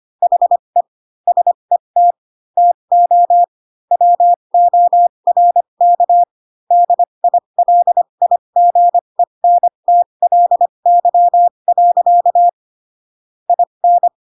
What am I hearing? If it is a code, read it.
Morse code, heard as a keyed tone, about 25 wpm, spaced as for 22 wpm, 700 Hz. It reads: HE SET TO WORK DILIGENTLY. IN